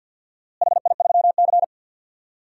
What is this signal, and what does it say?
Morse code, heard as a keyed tone, about 53 wpm, 700 Hz. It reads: HI3C
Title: DXCC-100 Call Signs - 4 Characters - Repeated 2x - 50wpm